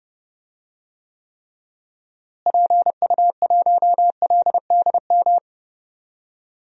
Morse code, heard as a keyed tone, about 30 wpm, 700 Hz. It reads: PU1LDM
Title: DXCC-100 Call Signs - 6 Characters - Repeated 3x - 30wpm